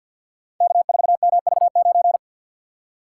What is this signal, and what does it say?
Morse code, heard as a keyed tone, about 47 wpm, 700 Hz. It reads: K4MV9
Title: Code Groups - 5 Letter:Numbers - 50wpm